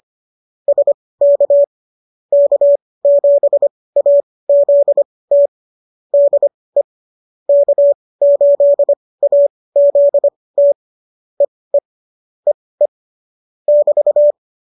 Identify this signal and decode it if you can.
Morse code, heard as a keyed tone, about 25 wpm, spaced as for 18 wpm, 570 Hz. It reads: SK K7AZT DE K8AZT EE EE <BT>